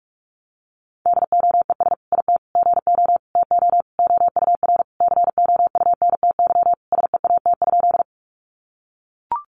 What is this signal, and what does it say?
Morse code, heard as a keyed tone, about 44 wpm, 700 Hz. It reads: DOES IT GO TO OUR COUNTY SEAT?
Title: Sentences from Top 1500 Words 45wpm